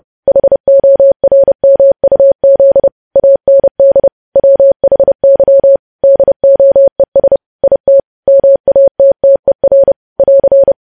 Morse code, heard as a keyed tone, about 30 wpm, 550 Hz. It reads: HORMUZ AND WHY DOES IT MATTER <AR>
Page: news headlines in morse code at 30 wpm